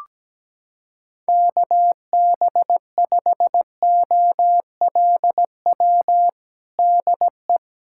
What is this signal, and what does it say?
Morse code, heard as a keyed tone, about 17 wpm, 700 Hz. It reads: KB5OLW DE
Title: USA Callsign DE Callsign 17wpm